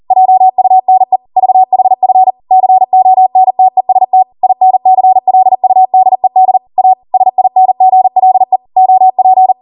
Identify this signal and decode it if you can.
Morse code, heard as a keyed tone, about 40 wpm, 750 Hz. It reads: JUNE VHF CONTEST INCLUDED A SINGLE OP